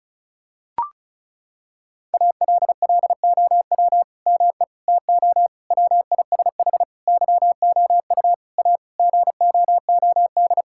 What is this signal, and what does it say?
Morse code, heard as a keyed tone, about 36 wpm, 700 Hz. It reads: ALLOW ME TO WISH YOU A GOOD
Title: Sentences from Top 1600 Words 35wpm